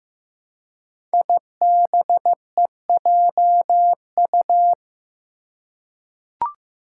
Morse code, heard as a keyed tone, about 15 wpm, 700 Hz. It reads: IBEJU
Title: Five Challenging Letter-Number Combinations 15wpm